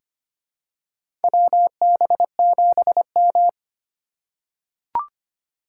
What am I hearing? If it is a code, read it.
Morse code, heard as a keyed tone, about 25 wpm, 700 Hz. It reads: WB7M
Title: USA Call Signs - 4 Characters - 2x Character Spacing - 25wpm